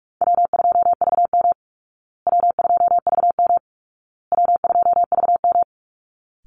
Morse code, heard as a keyed tone, about 44 wpm, 700 Hz. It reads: W2VM W2VM W2VM
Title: Call Signs Repeated 2x - 45wpm